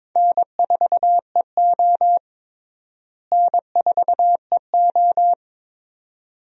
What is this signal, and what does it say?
Morse code, heard as a keyed tone, about 22 wpm, 700 Hz. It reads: N4EO N4EO
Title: USA Call Signs - 4 Characters - Repeated 2x - 22wpm